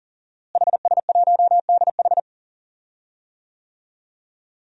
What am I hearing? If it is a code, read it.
Morse code, heard as a keyed tone, about 40 wpm, 700 Hz. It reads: HS1DH
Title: DXCC-100 Call Signs - 5 Characters - Repeated 2x - 40wpm